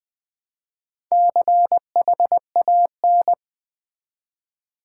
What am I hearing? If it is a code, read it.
Morse code, heard as a keyed tone, about 20 wpm, 700 Hz. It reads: CHAN
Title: Build A Word - 2 Letters At A Time - 6-Letter Words - 20wpm